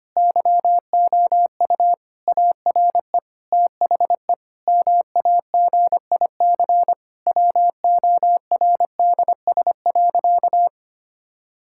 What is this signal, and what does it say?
Morse code, heard as a keyed tone, about 25 wpm, 700 Hz. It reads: YOU ARE THE MAGIC WORDS.
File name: practice / 25wpm / Difficult Sentences with Hints - 25wpm